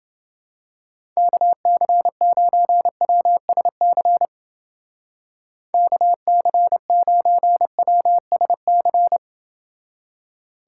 Morse code, heard as a keyed tone, about 29 wpm, 700 Hz. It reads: KC9WSC KC9WSC